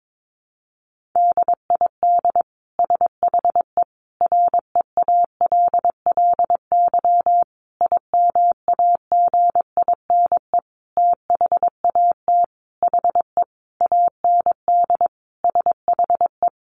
Morse code, heard as a keyed tone, about 22 wpm, 700 Hz. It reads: DID SHE REALLY IMAGINE THAT HE AND SHE
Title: Sentences from Top 1400 Words 22wpm